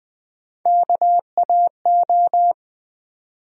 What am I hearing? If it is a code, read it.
Morse code, heard as a keyed tone, about 21 wpm, 700 Hz. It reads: KAO
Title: USA Call Sign Postfixes - 3 Letters 20wpm